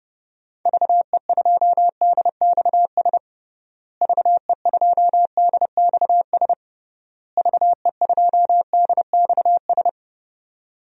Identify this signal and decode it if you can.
Morse code, heard as a keyed tone, about 30 wpm, 700 Hz. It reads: VE2DXS VE2DXS VE2DXS
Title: DXCC-100 Call Signs - 6 Characters - Repeated 3x - 30wpm